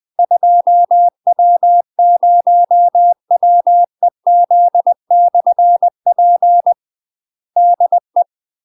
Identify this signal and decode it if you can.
Morse code, heard as a keyed tone, about 20 wpm, 700 Hz. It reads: 2W0WEZ/P DE
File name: UK MASTER QSOs TRAINING / 20WPM / UK MASTER QSOs 0065